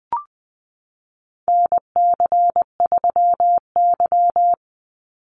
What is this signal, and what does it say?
Morse code, heard as a keyed tone, about 20 wpm, 700 Hz. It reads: NC3Y
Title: Call Signs Repeated 3x - 20wpm